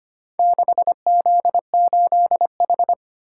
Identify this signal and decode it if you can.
Morse code, heard as a keyed tone, about 25 wpm, 700 Hz. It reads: 6Z8H
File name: practice / 25wpm / Four Letter-Number Combinations - 25wpm